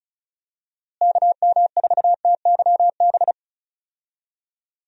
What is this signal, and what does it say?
Morse code, heard as a keyed tone, about 34 wpm, 700 Hz. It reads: KM4TYB
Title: USA Call Signs - 6 Characters - Repeated 2x - 35wpm